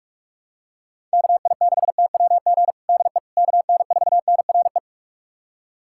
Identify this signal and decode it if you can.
Morse code, heard as a keyed tone, about 42 wpm, 700 Hz. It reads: KI6TWG DE KN4NRE
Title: USA Callsign DE Callsign - 6 Characters - 45wpm